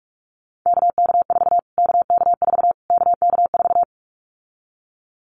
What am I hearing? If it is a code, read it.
Morse code, heard as a keyed tone, about 46 wpm, 700 Hz. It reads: KK4 KK4 KK4
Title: Callsign Prefixes for Top 25 DXCC - Repeated 3x - 45wpm